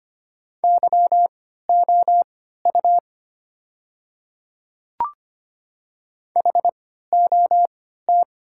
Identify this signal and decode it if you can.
Morse code, heard as a keyed tone, about 25 wpm, spaced as for 14 wpm, 700 Hz. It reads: YOU HOT
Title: 3x Character Spacing - Top 100 Words - 25wpm